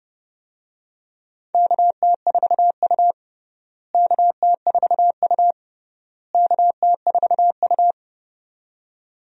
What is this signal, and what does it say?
Morse code, heard as a keyed tone, about 32 wpm, 700 Hz. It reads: KT4U KT4U KT4U
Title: USA Call Signs - 4 Characters - Repeated 3x - 30wpm